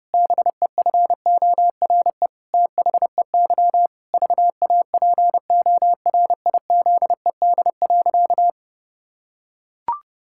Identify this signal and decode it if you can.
Morse code, heard as a keyed tone, about 32 wpm, 700 Hz. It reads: BEFORE THEY VAPORIZED.
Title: Pangram Challenge v2 - 30wpm